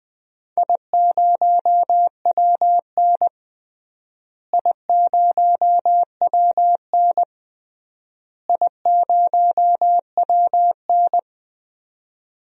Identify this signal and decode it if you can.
Morse code, heard as a keyed tone, about 20 wpm, 700 Hz. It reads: I0WN I0WN I0WN
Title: DXCC-100 Call Signs - 4 Characters - Repeated 3x - 20wpm